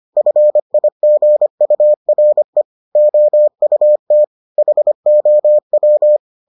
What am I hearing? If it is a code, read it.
Morse code, heard as a keyed tone, about 25 wpm, 600 Hz. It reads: FIGURE OUT HOW